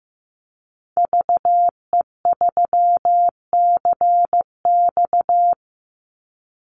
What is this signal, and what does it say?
Morse code, heard as a keyed tone, about 15 wpm, 700 Hz. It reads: VE3CX
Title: Call Signs Repeated 2x - 15wpm